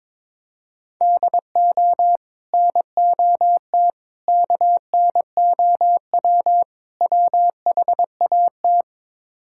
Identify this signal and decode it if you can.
Morse code, heard as a keyed tone, about 22 wpm, 700 Hz. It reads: DO NOT KNOW WHAT
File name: practice / 22wpm / Build A Phrase - 5-Word Phrases - 22wpm